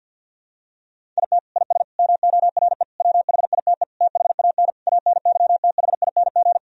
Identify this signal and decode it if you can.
Morse code, heard as a keyed tone, about 48 wpm, 700 Hz. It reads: IT IS MORE WHITE THAN ANYTHING
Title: Sentences from Top 1000 Words - Review Part 1 - 50wpm